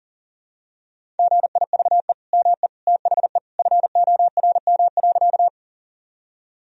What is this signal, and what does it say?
Morse code, heard as a keyed tone, about 40 wpm, 700 Hz. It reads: GIVE ME THE FORM.
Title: Sentences from Top 1000 Words - Review Part 1 - 40wpm